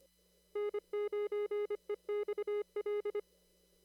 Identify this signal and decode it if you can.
Morse code, heard as a keyed tone, about 25 wpm, 410 Hz. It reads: N9EXL
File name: CW / kt5j wpx cw-op-k5pi